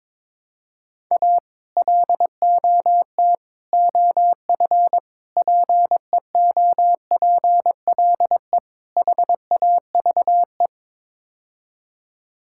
Morse code, heard as a keyed tone, about 22 wpm, 700 Hz. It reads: A LOT OF PEOPLE HAVE
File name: practice / 22wpm / Build A Phrase - 5-Word Phrases - 22wpm